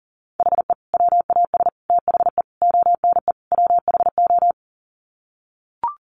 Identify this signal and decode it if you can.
Morse code, heard as a keyed tone, about 40 wpm, 700 Hz. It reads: HE WAS THE ONE WHO
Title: Sets of 5 Words Encore2 40wpm